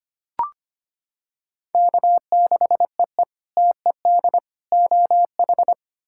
Morse code, heard as a keyed tone, about 25 wpm, 700 Hz. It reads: K6EE TED OH
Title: K1USN SST QSO Practice - 25wpm